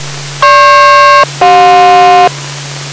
Aircraft SELCAL, heard as two pairs of tones. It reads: GP-CJ